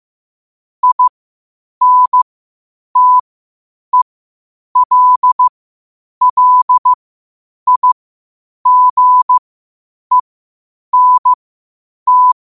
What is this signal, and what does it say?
Morse code, heard as a keyed tone, about 15 wpm, spaced as for 8 wpm, 1000 Hz. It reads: INTELLIGENT
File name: qotdfiles / 8wpm